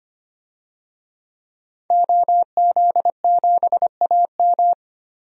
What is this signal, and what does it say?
Morse code, heard as a keyed tone, about 25 wpm, 700 Hz. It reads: OZ7AM
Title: DXCC-100 Call Signs - 5 Characters - 25wpm